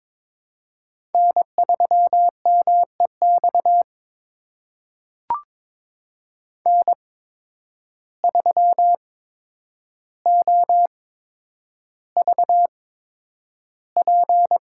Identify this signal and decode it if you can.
Morse code, heard as a keyed tone, about 22 wpm, spaced as for 9 wpm, 700 Hz. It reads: N3MEX N3OVP
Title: USA Call Signs - 5 Characters - 8x Character Spacing - 22wpm